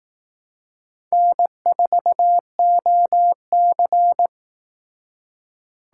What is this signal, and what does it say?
Morse code, heard as a keyed tone, about 18 wpm, 700 Hz. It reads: N4OC
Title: Speed Racing - USA 1x2 Call Signs 15wpm